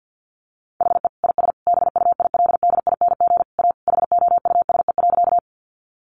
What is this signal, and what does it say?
Morse code, heard as a keyed tone, about 50 wpm, 700 Hz. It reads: HE IS BUILDING A HOUSE.
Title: Sentences from Top 700 Words 50wpm